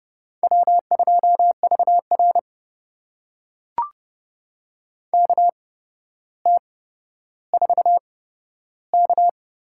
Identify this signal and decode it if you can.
Morse code, heard as a keyed tone, about 30 wpm, spaced as for 13 wpm, 700 Hz. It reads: W2VR KT4K
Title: USA Call Signs - 4 Characters - 8x Character Spacing - 30wpm